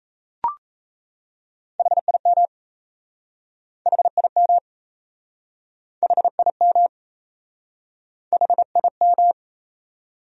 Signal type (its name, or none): Morse code